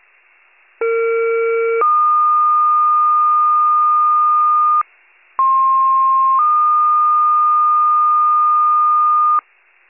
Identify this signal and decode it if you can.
Two-tone page, heard as a one-second tone, then a three-second tone, about 460 then 1220 Hz, about 1060 then 1220 Hz. It is heard from 0.8 to 9.4 s.